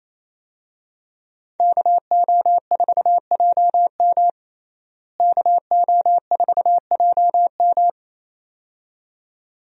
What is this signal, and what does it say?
Morse code, heard as a keyed tone, about 28 wpm, 700 Hz. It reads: KO4JM KO4JM
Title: USA Call Signs - 5 Characters - Repeated 2x - 28wpm